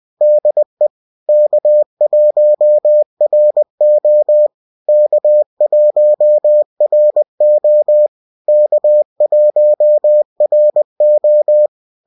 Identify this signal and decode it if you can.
Morse code, heard as a keyed tone, about 20 wpm, 600 Hz. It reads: DE K1RO K1RO K1RO